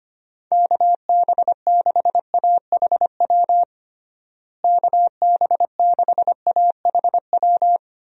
Morse code, heard as a keyed tone, about 25 wpm, 700 Hz. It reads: KB6AHW KB6AHW
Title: USA Call Signs - 6 Characters - Repeated 3x - 25wpm